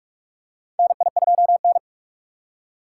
Morse code, heard as a keyed tone, about 46 wpm, 700 Hz. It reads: NI2N
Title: USA Call Signs - 4 Characters - 3x Character Spacing - 45wpm